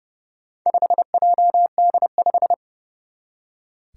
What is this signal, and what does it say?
Morse code, heard as a keyed tone, about 30 wpm, 700 Hz. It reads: 5JD5
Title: Four Challenging Letter-Number Combinations 30wpm